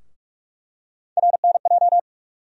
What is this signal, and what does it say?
Morse code, heard as a keyed tone, about 45 wpm, 700 Hz. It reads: RNJ
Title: USA Call Sign Postfixes - 3 Letters 45wpm